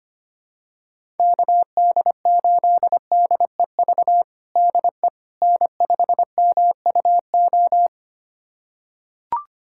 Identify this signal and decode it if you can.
Morse code, heard as a keyed tone, about 25 wpm, 700 Hz. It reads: KD8DEV DE N5MUO
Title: USA Callsign DE Callsign 25wpm